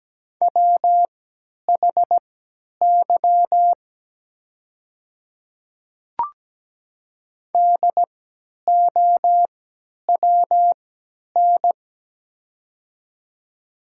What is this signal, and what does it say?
Morse code, heard as a keyed tone, about 17 wpm, spaced as for 10 wpm, 700 Hz. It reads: WHY DOWN
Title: 3x Character Spacing - Top 200 Words - 17wpm